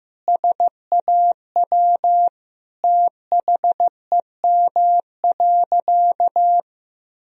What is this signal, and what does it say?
Morse code, heard as a keyed tone, about 15 wpm, 700 Hz. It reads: SAW THEM.
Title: Sentences from Top 1000 Words - Review Part 2 - 15wpm